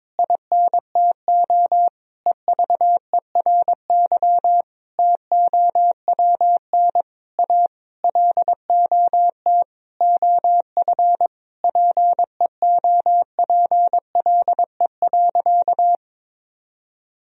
Morse code, heard as a keyed tone, about 22 wpm, 700 Hz. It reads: INTO EVERY TOWN A LOT OF PEOPLE.
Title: Sentences from Top 1000 Words - Review Part 3 - 22wpm